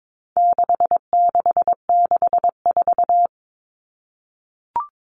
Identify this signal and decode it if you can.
Morse code, heard as a keyed tone, about 22 wpm, 700 Hz. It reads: 6664
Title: Numbers - 4-Digit 22wpm